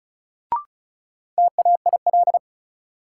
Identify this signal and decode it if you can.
Morse code, heard as a keyed tone, about 36 wpm, 700 Hz. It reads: TAIL
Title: Sentences from Top 1000 Words - Review Part 3 - New Word Drill - 35wpm